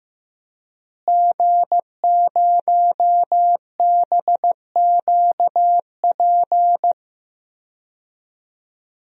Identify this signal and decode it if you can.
Morse code, heard as a keyed tone, about 16 wpm, 700 Hz. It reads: G0BQP
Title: DXCC-100 Call Signs - 5 Characters - Repeated 3x - 15wpm